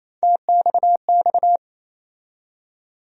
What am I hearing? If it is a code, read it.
Morse code, heard as a keyed tone, about 28 wpm, 700 Hz. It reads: TXX
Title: USA Call Sign Postfixes - 3 Letters 28wpm